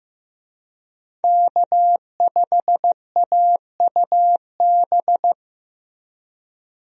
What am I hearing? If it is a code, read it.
Morse code, heard as a keyed tone, about 15 wpm, 700 Hz. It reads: K5AUB
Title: USA Call Signs - 5 Characters - Repeated 2x - 15wpm